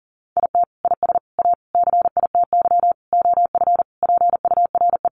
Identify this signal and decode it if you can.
Morse code, heard as a keyed tone, about 40 wpm, 700 Hz. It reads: IT IS A CITY OF PURE
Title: Sentences from Top 1500 Words 40wpm